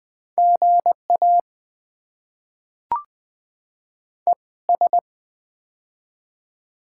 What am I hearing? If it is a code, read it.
Morse code, heard as a keyed tone, about 20 wpm, spaced as for 17 wpm, 690 Hz. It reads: GA ES
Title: Top 20 QSO Elements - 2x Character Spacing - 20wpm